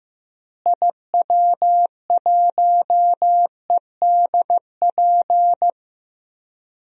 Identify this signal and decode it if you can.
Morse code, heard as a keyed tone, about 15 wpm, 700 Hz. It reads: IW1EDP
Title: DXCC-100 Call Signs - 6 Characters - Repeated 3x - 15wpm